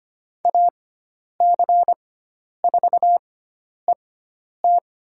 Morse code, heard as a keyed tone, about 25 wpm, spaced as for 10 wpm, 700 Hz. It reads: AC4ET
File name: Lesson 16g - Letter c - Call Signs 25wpm